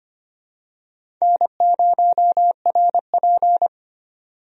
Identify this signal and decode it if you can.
Morse code, heard as a keyed tone, about 25 wpm, 700 Hz. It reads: N0RP
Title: USA Call Signs - 4 Characters - 25wpm